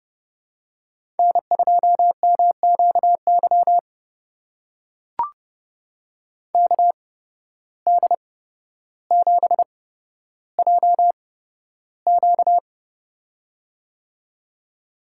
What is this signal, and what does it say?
Morse code, heard as a keyed tone, about 30 wpm, spaced as for 13 wpm, 700 Hz. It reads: N2MQY KD7JQ
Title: USA Call Signs - 5 Characters - 8x Character Spacing - 30wpm